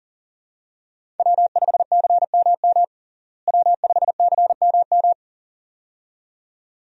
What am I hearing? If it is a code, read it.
Morse code, heard as a keyed tone, about 40 wpm, 700 Hz. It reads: W5CMM W5CMM
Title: USA Call Signs - 5 Characters - Repeated 2x - 40wpm